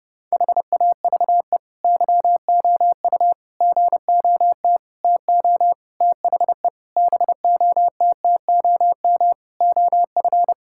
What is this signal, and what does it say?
Morse code, heard as a keyed tone, about 30 wpm, 700 Hz. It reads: HAVE YOU GOT TO THE BOTTOM OF
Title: Sentences from Top 1000 Words - Review Part 6 - 30wpm